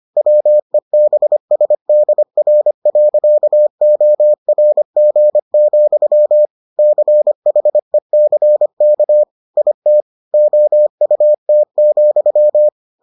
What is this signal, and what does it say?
Morse code, heard as a keyed tone, about 25 wpm, 600 Hz. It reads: WEBSDR.ORG, CHECK IT OUT,